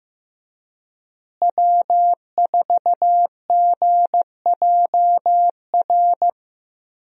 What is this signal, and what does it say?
Morse code, heard as a keyed tone, about 15 wpm, 700 Hz. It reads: W4GJR
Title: USA Call Signs - 5 Characters - Repeated 3x - 15wpm